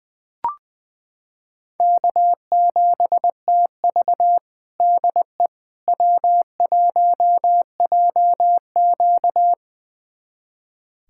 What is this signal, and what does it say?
Morse code, heard as a keyed tone, about 21 wpm, 700 Hz. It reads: K7TV DE W1JQ
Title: USA Callsign DE Callsign - 4 Characters - 20wpm